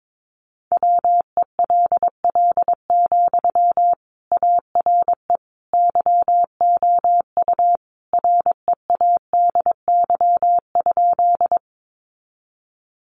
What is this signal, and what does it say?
Morse code, heard as a keyed tone, about 22 wpm, 700 Hz. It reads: WELL, ARE YOU READY?
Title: Sentences from Top 400 Words 22wpm